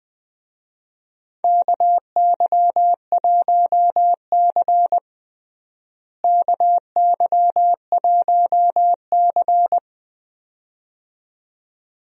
Morse code, heard as a keyed tone, about 20 wpm, 700 Hz. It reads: KY1C KY1C
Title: USA Call Signs - 4 Characters - Repeated 2x - 20wpm